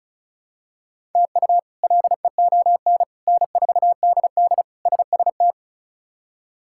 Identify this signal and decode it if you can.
Morse code, heard as a keyed tone, about 36 wpm, 700 Hz. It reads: TU LEON N4DD SST
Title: K1USN SST QSO Practice - 35wpm